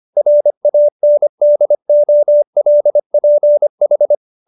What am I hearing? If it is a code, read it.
Morse code, heard as a keyed tone, about 25 wpm, 590 Hz. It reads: RANDOLPH
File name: QSO220 25